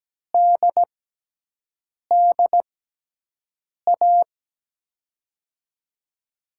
Morse code, heard as a keyed tone, about 17 wpm, spaced as for 6 wpm, 700 Hz. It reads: DDA
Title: USA Call Signs - 6 Characters - 6x Character Spacing - 17wpm